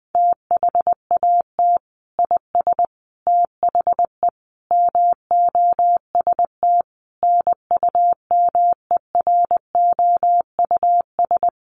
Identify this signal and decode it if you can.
Morse code, heard as a keyed tone, about 20 wpm, 700 Hz. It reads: THAT IS THE MOST NUMEROUS
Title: Sentences from Top 1400 Words 20wpm